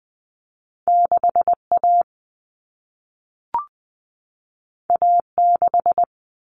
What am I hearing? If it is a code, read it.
Morse code, heard as a keyed tone, about 20 wpm, 700 Hz. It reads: 6 A A 6
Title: All Cut Numbers - 2-Digit 20wpm